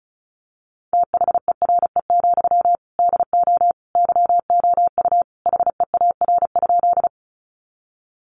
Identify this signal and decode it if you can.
Morse code, heard as a keyed tone, about 34 wpm, 700 Hz. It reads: THERE, DO YOU HEAR?
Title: Sentences from Top 300 Words 35wpm